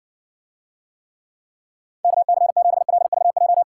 Morse code, heard as a keyed tone, about 60 wpm, 700 Hz. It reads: KC7LVJ